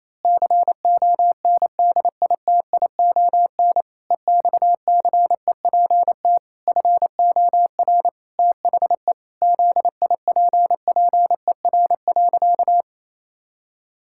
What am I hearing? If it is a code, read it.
Morse code, heard as a keyed tone, about 28 wpm, 700 Hz. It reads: CONDITION EXCEPT FOR THE ZIPPER.
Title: Pangram Challenge v2 - 28wpm